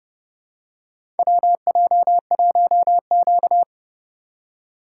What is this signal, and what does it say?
Morse code, heard as a keyed tone, about 31 wpm, 700 Hz. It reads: WJ1Q
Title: USA Call Signs - 4 Characters - 2x Character Spacing - 30wpm